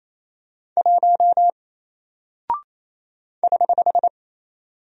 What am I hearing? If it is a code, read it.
Morse code, heard as a keyed tone, about 28 wpm, 700 Hz. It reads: 1 <HH>
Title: Single Character (Letter:Number:Punctuation:Prosign) Alt 28wpm